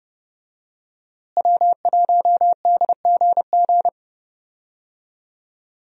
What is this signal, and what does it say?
Morse code, heard as a keyed tone, about 30 wpm, 700 Hz. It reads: W1DGG
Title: USA Call Signs - 5 Characters - Repeated 3x - 30wpm